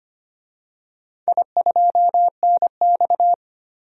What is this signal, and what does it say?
Morse code, heard as a keyed tone, about 25 wpm, 700 Hz. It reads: I2NX
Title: DXCC-100 Call Signs - 4 Characters - 25wpm